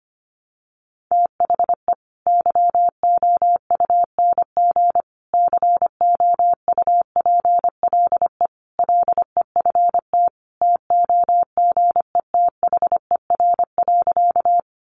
Morse code, heard as a keyed tone, about 25 wpm, 700 Hz. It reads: THE YOUNG COUPLE LEFT TOGETHER.
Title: Sentences from Top 1100 Words 25wpm